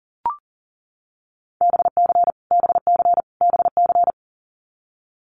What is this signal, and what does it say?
Morse code, heard as a keyed tone, about 40 wpm, 700 Hz. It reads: BC BC BC
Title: Canadian Provinces and Territories - Repeated 3x - 40wpm